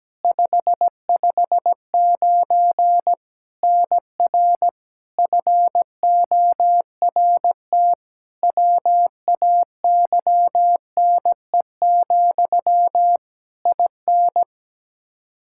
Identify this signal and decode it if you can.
Morse code, heard as a keyed tone, about 17 wpm, 700 Hz. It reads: 559 NR FORT WAYNE, IN
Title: Practice QSO - Exchange 1 - Beginning - 17wpm